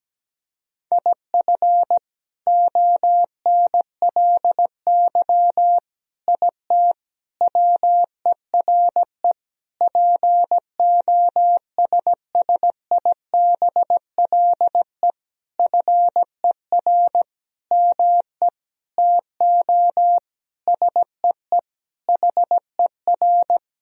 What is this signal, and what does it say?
Morse code, heard as a keyed tone, about 17 wpm, 700 Hz. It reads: IF ONLY IT WERE POSSIBLE FER ME TO SEE HER
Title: Sentences from Top 500 Words and Abbreviations 17wpm